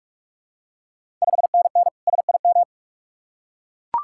Morse code, heard as a keyed tone, about 41 wpm, 700 Hz. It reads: 5NN SIM
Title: Ontario QSO Party Exchange 45wpm